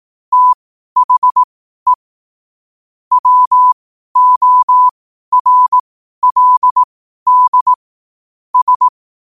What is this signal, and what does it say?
Morse code, heard as a keyed tone, about 18 wpm, spaced as for 12 wpm, 1000 Hz. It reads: THE WORLD S
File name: qotdfiles / 12wpm